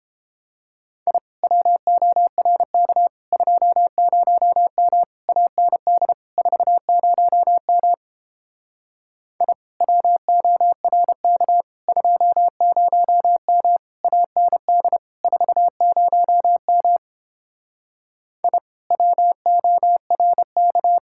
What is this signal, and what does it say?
Morse code, heard as a keyed tone, about 29 wpm, 700 Hz. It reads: I WORK 20M AND 40M I WORK 20M AND 40M I WORK